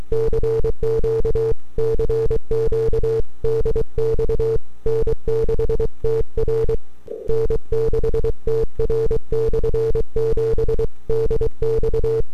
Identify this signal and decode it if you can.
Morse code, heard as a keyed tone, about 23 wpm, 450 Hz. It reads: CQ CQ DX N6TR N6TR/7 DX
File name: n6tr / 12Oct2009-160